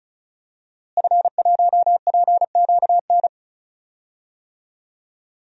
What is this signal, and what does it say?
Morse code, heard as a keyed tone, about 36 wpm, 700 Hz. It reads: F1PQN
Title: DXCC-100 Call Signs - 5 Characters - Repeated 3x - 35wpm